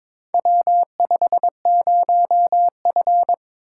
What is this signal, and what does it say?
Morse code, heard as a keyed tone, about 22 wpm, 700 Hz. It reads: W50F
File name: practice / 22wpm / Four Letter-Number Combinations - 22wpm